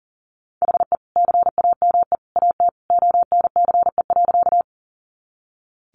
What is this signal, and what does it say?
Morse code, heard as a keyed tone, about 40 wpm, 700 Hz. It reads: HE CAME AT ONCE.
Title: Sentences from Top 300 Words 40wpm